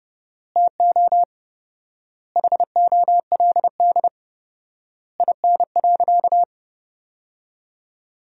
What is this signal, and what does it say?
Morse code, heard as a keyed tone, about 30 wpm, 700 Hz. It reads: TO HOLD IN.
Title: 4x Word-Spacing - Sentences from Top 400 Words - 30wpm